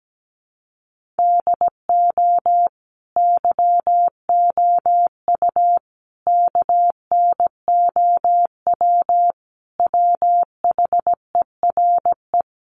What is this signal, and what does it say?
Morse code, heard as a keyed tone, about 17 wpm, 700 Hz. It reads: DO YOU KNOW WHERE
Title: Sentences from Top 1400 Words 17wpm